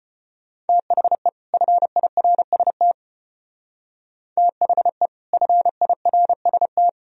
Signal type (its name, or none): Morse code